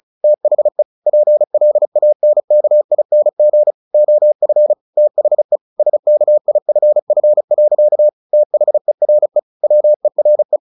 Morse code, heard as a keyed tone, about 34 wpm, 600 Hz. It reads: THE PLANKING OF THE SKIFF. THERE WERE
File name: the old man and the sea 2 35wpm